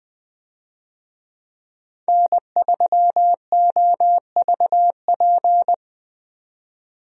Morse code, heard as a keyed tone, about 20 wpm, 700 Hz. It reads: N3OVP
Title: USA Call Signs - 5 Characters - 20wpm